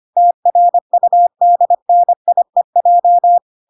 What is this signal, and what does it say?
Morse code, heard as a keyed tone, about 25 wpm, 700 Hz. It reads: TRUDNIEJ
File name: 25wpm